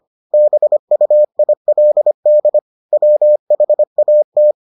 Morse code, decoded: BUILD WHAT

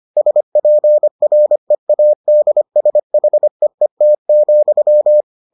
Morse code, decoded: SPREADSHEET,

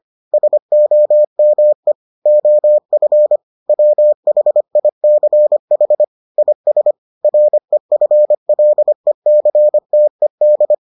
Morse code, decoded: SOME OF WHICH IS REFLECTED

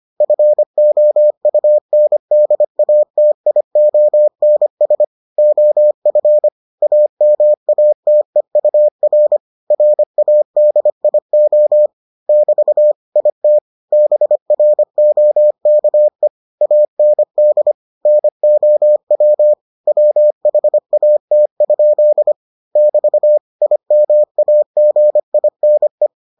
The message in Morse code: FOUNDATIONS OF AMATEUR RADIO <BT> IT BROKE AND NOW WHAT? <BT> IMAGINE

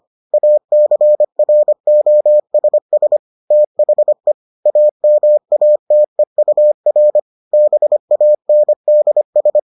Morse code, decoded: ACROSS THE AMATEUR BANDS

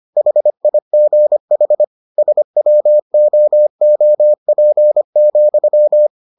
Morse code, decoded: HIGH SWOOP,

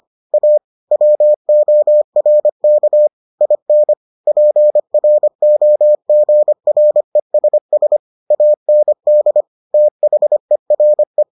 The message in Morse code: A WORK IN PROGRESS AND THERE

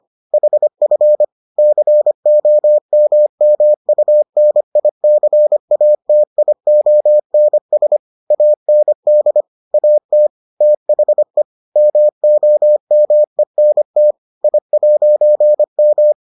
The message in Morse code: HF COMMUNICATIONS AND AT THE MOMENT I'M

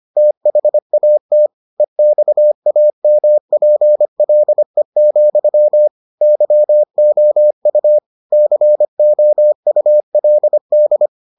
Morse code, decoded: THAT EXAMPLE, YOU COULD

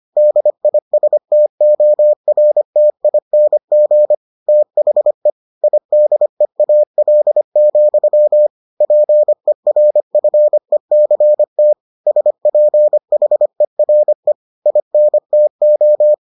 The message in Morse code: DISTORTING THE IDEAL, PERFECT SPHERE INTO

